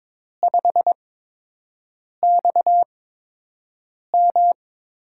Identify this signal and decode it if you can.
Morse code, heard as a keyed tone, about 22 wpm, spaced as for 6 wpm, 700 Hz. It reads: 5XM
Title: USA Call Signs - 8x Character Spacing - 22wpm